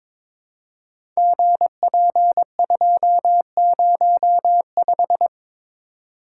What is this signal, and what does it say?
Morse code, heard as a keyed tone, about 22 wpm, 700 Hz. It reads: GP205
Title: Code Groups - 5 Letter:Numbers - 22wpm